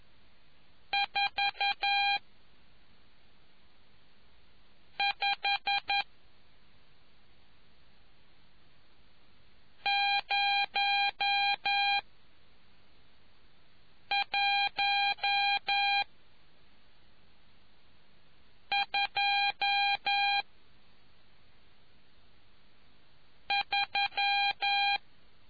Morse code, heard as a keyed tone, about 11 wpm, 800 Hz. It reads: 4 5 0 1 2 3